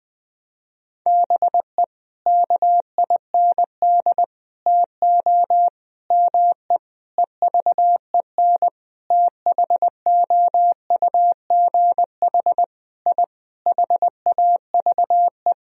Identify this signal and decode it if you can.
Morse code, heard as a keyed tone, about 20 wpm, 700 Hz. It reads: BE KIND TO ME EVEN THOUGH I HAVE
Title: Sentences from Top 1000 Words - Review Part 2 - 20wpm